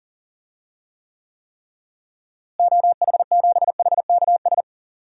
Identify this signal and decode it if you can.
Morse code, heard as a keyed tone, about 40 wpm, 700 Hz. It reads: OH7HKS
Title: DXCC-100 Call Signs - 6 Characters - Repeated 3x - 40wpm